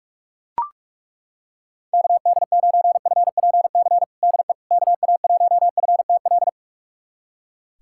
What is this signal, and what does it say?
Morse code, heard as a keyed tone, about 41 wpm, 700 Hz. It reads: KD9FPC DE KA1FTL
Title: USA Callsign DE Callsign - 6 Characters - 45wpm